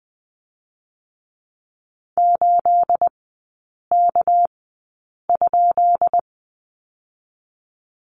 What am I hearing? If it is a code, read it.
Morse code, heard as a keyed tone, about 19 wpm, 700 Hz. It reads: 8 K ?
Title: 212 Lesson 34e - Number 8 - Three Characters 20wpm